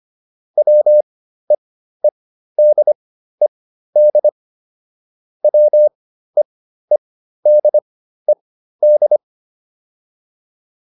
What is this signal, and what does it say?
Morse code, heard as a keyed tone, about 25 wpm, spaced as for 13 wpm, 600 Hz. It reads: WEEDED WEEDED